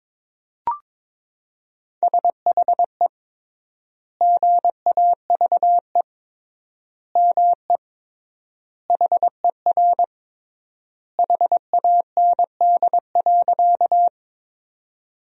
Morse code, heard as a keyed tone, about 21 wpm, 700 Hz. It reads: SHE GAVE ME HER HAND.
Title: 3x Word-Spacing - Sentences from Top 500 Words - 22wpm